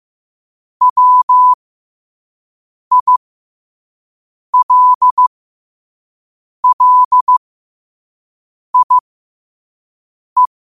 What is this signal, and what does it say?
Morse code, heard as a keyed tone, about 15 wpm, spaced as for 5 wpm, 1000 Hz. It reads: WILLIE
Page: qotd-at-5-wpm